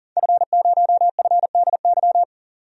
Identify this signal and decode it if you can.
Morse code, heard as a keyed tone, about 39 wpm, 700 Hz. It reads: F0FDY
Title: DXCC-100 Call Signs - 5 Characters - 40wpm